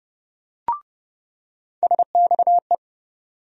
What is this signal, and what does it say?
Morse code, heard as a keyed tone, about 30 wpm, 700 Hz. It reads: SXE
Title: USA Call Sign Postfixes - 3 Letters 30wpm